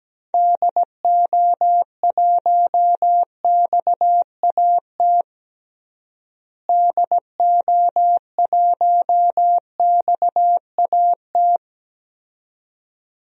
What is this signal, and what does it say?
Morse code, heard as a keyed tone, about 17 wpm, 700 Hz. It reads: DO1XAT DO1XAT
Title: DXCC-100 Call Signs - 6 Characters - Repeated 2x - 17wpm